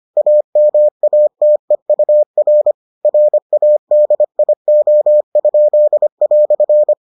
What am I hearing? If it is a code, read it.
Morse code, heard as a keyed tone, about 25 wpm, 600 Hz. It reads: AMATEUR RADIO?"